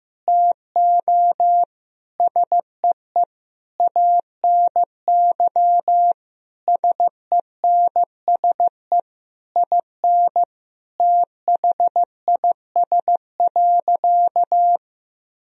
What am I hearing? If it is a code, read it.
Morse code, heard as a keyed tone, about 15 wpm, 700 Hz. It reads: TO SEE ANY SENSE IN THIS.
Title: Sentences from Top 1000 Words - Review Part 4 - 15wpm